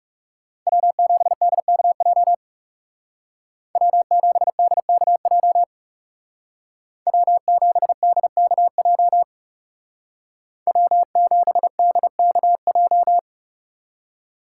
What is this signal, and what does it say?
Morse code, heard as a keyed tone, about 37 wpm, 700 Hz. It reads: W7DKJ W7DKJ W7DKJ W7DKJ
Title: Speed Racing - USA 1x3 Call Signs 30wpm